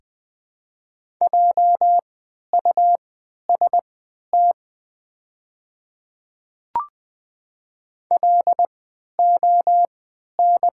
Morse code, heard as a keyed tone, about 20 wpm, spaced as for 11 wpm, 700 Hz. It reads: JUST LON
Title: 3x Character Spacing - Top 200 Words - 20wpm